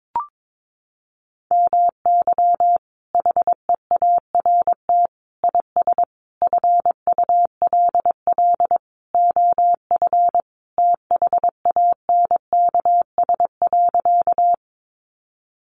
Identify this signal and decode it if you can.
Morse code, heard as a keyed tone, about 22 wpm, 700 Hz. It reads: MY HEART IS FULL OF THANKS.
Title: Sentences from Top 1200 Words 22wpm